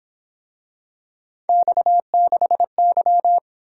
Morse code, heard as a keyed tone, about 26 wpm, 700 Hz. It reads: X6Y
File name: Speed-Racing - Three Letter-Number Combinations - 22wpm